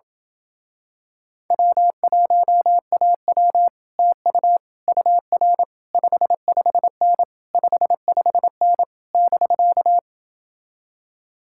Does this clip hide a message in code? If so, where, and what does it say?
Yes: Morse code from 1.5 to 10.0 s, heard as a keyed tone, about 28 wpm, 700 Hz. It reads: W1AW TU UR 55N 55N <BK>